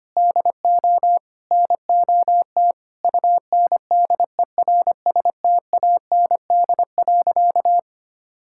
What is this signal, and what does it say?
Morse code, heard as a keyed tone, about 25 wpm, 700 Hz. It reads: DO NOT UNDERSTAND.
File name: practice / 25wpm / Sentences with 3-Word Phrases - 25wpm